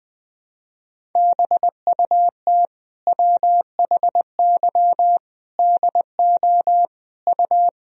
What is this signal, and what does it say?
Morse code, heard as a keyed tone, about 20 wpm, 700 Hz. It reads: BUT WHY DO U
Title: Sentences from Top 200 Words and Abbreviations 20wpm